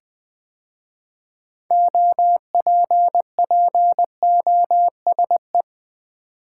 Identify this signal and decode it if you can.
Morse code, heard as a keyed tone, about 20 wpm, 700 Hz. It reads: OPPOSE
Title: 6 Letter Words - First 3 Letters Given - 20wpm